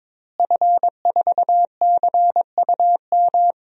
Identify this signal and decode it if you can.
Morse code, heard as a keyed tone, about 22 wpm, 700 Hz. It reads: F4CUM
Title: DXCC-100 Call Signs - 5 Characters - Repeated 2x - 22wpm